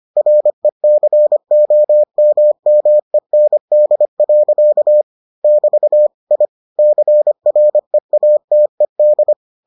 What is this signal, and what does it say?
Morse code, heard as a keyed tone, about 25 wpm, 600 Hz. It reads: RECOMMEND. <BT> I CREATED